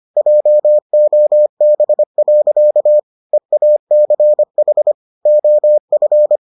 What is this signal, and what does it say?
Morse code, heard as a keyed tone, about 25 wpm, 600 Hz. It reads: JOB. EACH OF